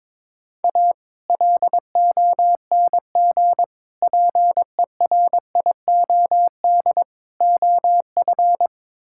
Morse code, heard as a keyed tone, about 22 wpm, 700 Hz. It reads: A LONG PERIOD OF
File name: practice / 22wpm / Build A Phrase - 5-Word Phrases - 22wpm